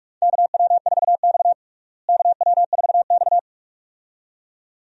Morse code, heard as a keyed tone, about 42 wpm, 700 Hz. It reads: KW4X KW4X
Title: USA Call Signs - 4 Characters - Repeated 2x - 45wpm